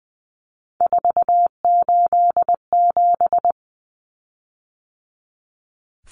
Morse code, heard as a keyed tone, about 20 wpm, 700 Hz. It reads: 487
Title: Common Cut Numbers - 3-Digit 20wpm